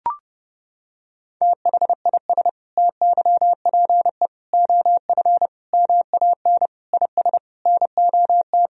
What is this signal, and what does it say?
Morse code, heard as a keyed tone, about 30 wpm, 700 Hz. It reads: THIS TYPE OF MAN IS NOT
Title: Sentences from Top 1000 Words - Review Part 3 - 30wpm